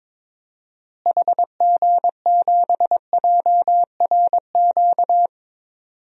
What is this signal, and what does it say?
Morse code, heard as a keyed tone, about 21 wpm, 700 Hz. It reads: HG7JRQ
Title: DXCC-100 Call Signs - 6 Characters - Repeated 2x - 22wpm